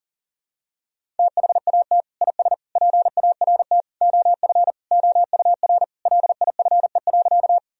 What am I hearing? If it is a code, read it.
Morse code, heard as a keyed tone, about 40 wpm, 700 Hz. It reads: THAT IS PART OF OUR LIFE.
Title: Sentences from Top 1000 Words - Review Part 1 - 40wpm